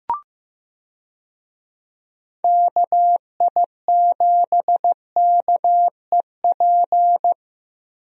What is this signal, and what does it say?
Morse code, heard as a keyed tone, about 15 wpm, 700 Hz. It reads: KI7KEP
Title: USA Call Signs - 6 Characters - 15wpm